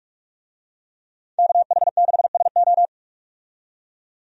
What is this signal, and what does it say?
Morse code, heard as a keyed tone, about 46 wpm, 700 Hz. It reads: KH6SO
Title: DXCC-100 Call Signs - 5 Characters - 45wpm